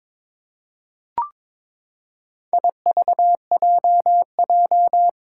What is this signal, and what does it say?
Morse code, heard as a keyed tone, about 22 wpm, 700 Hz. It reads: IVJJ
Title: Four Challenging Letter-Number Combinations 22wpm